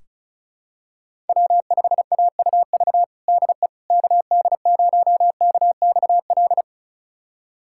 Morse code, heard as a keyed tone, about 36 wpm, 700 Hz. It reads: W5AUV DE KD0KXL
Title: USA Callsign DE Callsign 35wpm